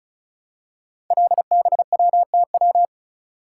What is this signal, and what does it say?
Morse code, heard as a keyed tone, about 36 wpm, 700 Hz. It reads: LBWTW
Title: Code Groups - 5 Letter:Numbers - 35wpm